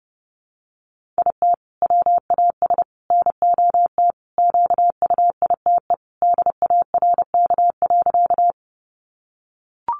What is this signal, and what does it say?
Morse code, heard as a keyed tone, about 30 wpm, 700 Hz. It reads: IT WAS NOT QUITE DARK.
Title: Sentences from Top 700 Words 30wpm